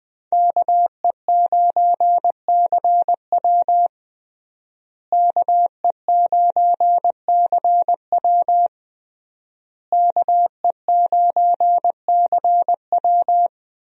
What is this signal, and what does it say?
Morse code, heard as a keyed tone, about 21 wpm, 700 Hz. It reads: KE9CW KE9CW KE9CW
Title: USA Call Signs - 5 Characters - Repeated 3x - 20wpm